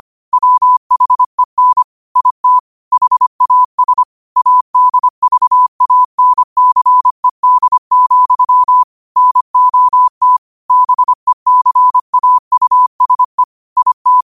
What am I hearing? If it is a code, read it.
Morse code, heard as a keyed tone, about 25 wpm, 1000 Hz. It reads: WHEN IT HAS ADVANCED, NOT BECAUSE IT
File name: qotdfiles / 25wpm